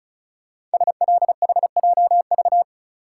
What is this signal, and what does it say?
Morse code, heard as a keyed tone, about 36 wpm, 700 Hz. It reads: SLHJV
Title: Code Groups - 5 Letters - 35wpm